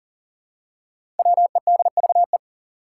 Morse code, heard as a keyed tone, about 40 wpm, 700 Hz. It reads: WEDVE
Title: Five Challenging Letter-Number Combinations 40wpm